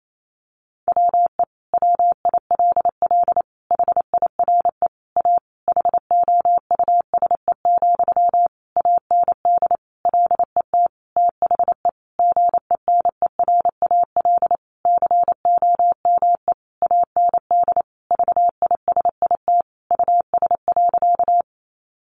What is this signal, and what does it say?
Morse code, heard as a keyed tone, about 28 wpm, 700 Hz. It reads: WE WILL HIRE A HOUSE, AND LET THE GENERAL COME AND VISIT US.
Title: Sentences from Top 1400 Words 28wpm